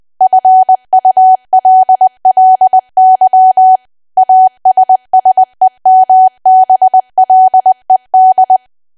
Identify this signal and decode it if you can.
Morse code, heard as a keyed tone, about 20 wpm, 750 Hz. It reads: FULLY ASSEMBLED
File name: cwa / W1AW-Level3 / 20WPM